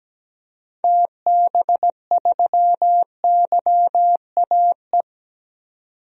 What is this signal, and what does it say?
Morse code, heard as a keyed tone, about 17 wpm, 700 Hz. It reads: TB3YAE
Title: DXCC-100 Call Signs - 6 Characters - 17wpm